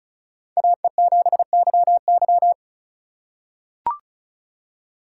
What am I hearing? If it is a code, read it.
Morse code, heard as a keyed tone, about 36 wpm, 700 Hz. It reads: AE7YY
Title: USA Call Signs - 4x Character Spacing - 35wpm